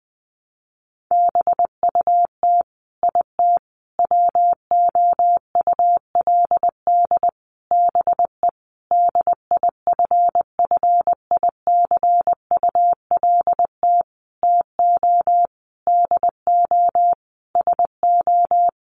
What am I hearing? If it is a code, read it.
Morse code, heard as a keyed tone, about 20 wpm, 700 Hz. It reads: BUT IT WOULD BE DIFFICULT TO DO SO
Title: Sentences from Top 600 Words 20wpm